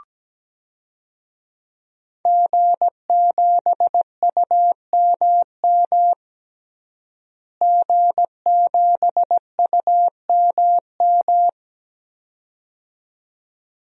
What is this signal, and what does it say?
Morse code, heard as a keyed tone, about 17 wpm, 700 Hz. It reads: G7UMM G7UMM
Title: DXCC-100 Call Signs - 5 Characters - Repeated 2x - 17wpm